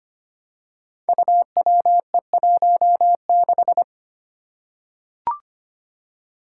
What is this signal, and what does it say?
Morse code, heard as a keyed tone, about 25 wpm, 700 Hz. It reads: UWE16
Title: Five Challenging Letter-Number Combinations 25wpm